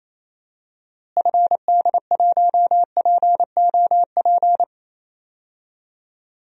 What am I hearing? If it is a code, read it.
Morse code, heard as a keyed tone, about 28 wpm, 700 Hz. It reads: FD1POP
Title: DXCC-100 Call Signs - 6 Characters - Repeated 2x - 28wpm